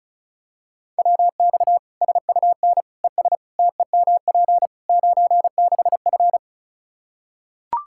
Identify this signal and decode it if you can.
Morse code, heard as a keyed tone, about 34 wpm, 700 Hz. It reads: WX SUN ES TEMP 96F
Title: Practice QSO - Exchange 2 - Info - 35wpm